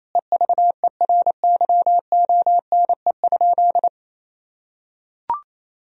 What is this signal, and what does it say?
Morse code, heard as a keyed tone, about 27 wpm, 700 Hz. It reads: EVERYONE?
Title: Sentences from Top 1000 Words - Review Part 4 - 28wpm